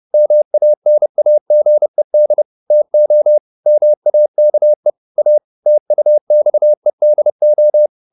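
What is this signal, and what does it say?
Morse code, heard as a keyed tone, about 30 wpm, 600 Hz. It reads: MANAGED TO MAKE A TUXEDO
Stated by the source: Morse code at 30 wpm, the novel Never by Ken Follett in English, from CW med Gnister.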